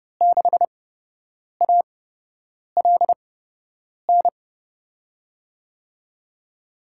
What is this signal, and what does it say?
Morse code, heard as a keyed tone, about 32 wpm, spaced as for 8 wpm, 700 Hz. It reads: 6ALN
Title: USA Call Signs - 6 Characters - 8x Character Spacing - 30wpm